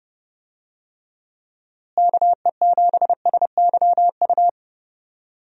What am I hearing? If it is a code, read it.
Morse code, heard as a keyed tone, about 30 wpm, 700 Hz. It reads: KE7SYU
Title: USA Call Signs - 6 Characters - Repeated 2x - 30wpm